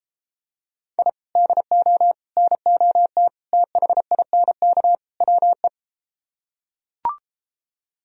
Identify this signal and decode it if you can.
Morse code, heard as a keyed tone, about 33 wpm, 700 Hz. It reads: I DO NOT THINK WE